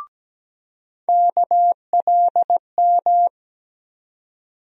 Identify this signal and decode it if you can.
Morse code, heard as a keyed tone, about 17 wpm, 700 Hz. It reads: KLM
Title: USA Call Sign Postfixes - 3 Letters 17wpm